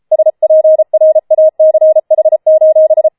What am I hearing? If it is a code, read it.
Morse code, heard as a keyed tone, about 33 wpm, 630 Hz. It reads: SPRACH:
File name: Schneewittchen Teil5 34wpm